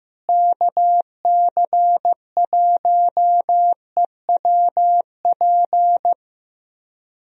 Morse code, heard as a keyed tone, about 15 wpm, 700 Hz. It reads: KC1EWP
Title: USA Call Signs - 6 Characters - Repeated 2x - 15wpm